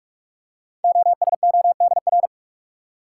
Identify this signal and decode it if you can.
Morse code, heard as a keyed tone, about 44 wpm, 700 Hz. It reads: OSODR
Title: Code Groups - 5 Letters - 45wpm